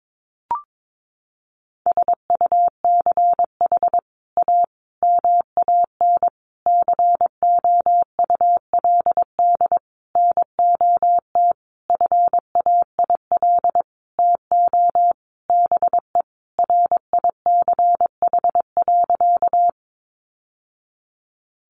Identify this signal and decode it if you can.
Morse code, heard as a keyed tone, about 22 wpm, 700 Hz. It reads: SUCH A MAN COULD NOT FAIL TO BE RICH.
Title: Sentences from Top 1100 Words 22wpm